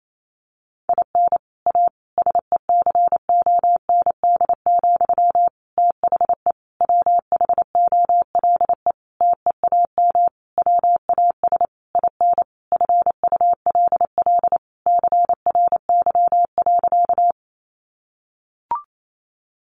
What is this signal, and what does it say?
Morse code, heard as a keyed tone, about 28 wpm, 700 Hz. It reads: IN A SECOND, THE WHOLE TEAM WAS IN FULL CRY.
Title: Sentences from Top 1500 Words 28wpm